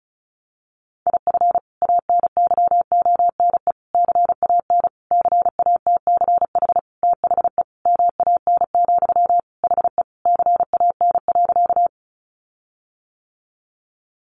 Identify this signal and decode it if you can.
Morse code, heard as a keyed tone, about 34 wpm, 700 Hz. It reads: IF ANYONE CAN CATCH THE MAN, HE CAN.
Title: Sentences from Top 1500 Words 35wpm